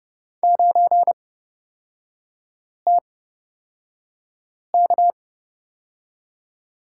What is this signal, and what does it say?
Morse code, heard as a keyed tone, about 30 wpm, spaced as for 5 wpm, 700 Hz. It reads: 9TK
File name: Lesson 36e - BT - Three Characters 30wpm